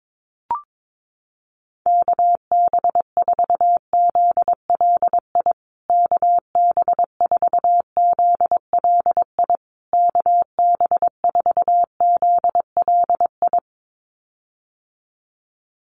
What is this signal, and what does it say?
Morse code, heard as a keyed tone, about 22 wpm, 700 Hz. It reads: KB4ZLI KB4ZLI KB4ZLI
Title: USA Call Signs Repeated 3x 22wpm